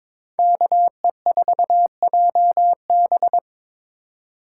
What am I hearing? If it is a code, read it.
Morse code, heard as a keyed tone, about 26 wpm, 700 Hz. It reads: KE4JB
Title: Speed Racing - USA 2x2 Call Signs 22wpm